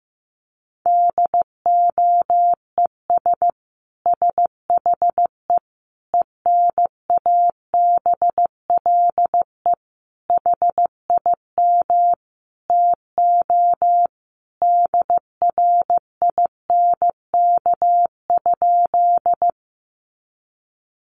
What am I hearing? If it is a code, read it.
Morse code, heard as a keyed tone, about 15 wpm, 700 Hz. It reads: DOES SHE ENABLE HIM TO DRINK?
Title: Sentences from Top 1500 Words 15wpm